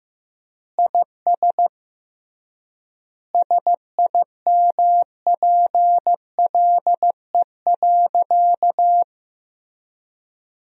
Morse code, heard as a keyed tone, about 15 wpm, 700 Hz. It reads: IS SIMPLE.
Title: 3x Word-Spacing - Sentences from Top 400 Words - 15wpm